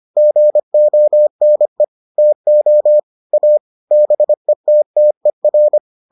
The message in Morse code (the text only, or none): GONE TO A BETTER